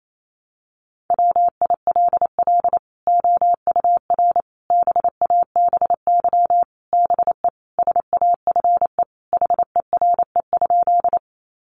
Morse code, heard as a keyed tone, about 28 wpm, 700 Hz. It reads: WILL OUR BABY BE SAFE HERE?
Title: Sentences from Top 1300 Words 28wpm